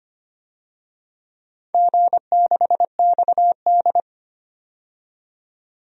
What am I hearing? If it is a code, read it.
Morse code, heard as a keyed tone, about 25 wpm, 700 Hz. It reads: G6XD
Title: DXCC-100 Call Signs - 4 Characters - 25wpm